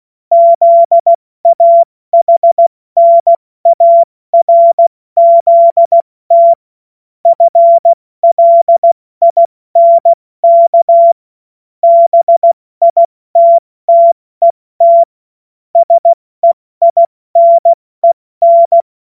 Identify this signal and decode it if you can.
Morse code, heard as a keyed tone, about 16 wpm, 680 Hz. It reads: ZAHNARZT FLINK BITTET SEINEN
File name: Morsekurs / Witze / Witz 25. Juli 2023 70 F80